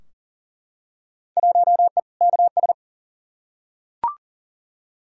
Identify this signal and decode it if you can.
Morse code, heard as a keyed tone, about 40 wpm, 700 Hz. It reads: 1E KS
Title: Field Day Exchanges - 40wpm